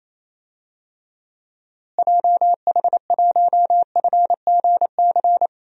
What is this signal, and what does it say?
Morse code, heard as a keyed tone, about 28 wpm, 700 Hz. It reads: JH1FGC
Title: DXCC-100 Call Signs - 6 Characters - Repeated 3x - 28wpm